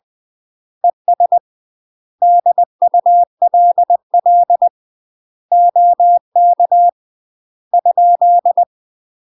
Morse code, heard as a keyed tone, about 20 wpm, 700 Hz. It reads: ES DULL OK ?
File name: UK MASTER QSOs TRAINING / 20WPM / UK MASTER QSOs 0020